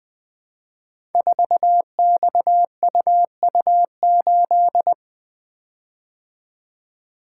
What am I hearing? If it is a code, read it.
Morse code, heard as a keyed tone, about 20 wpm, 700 Hz. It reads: 4XUU8
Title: Code Groups - 5 Letter:Numbers - 20wpm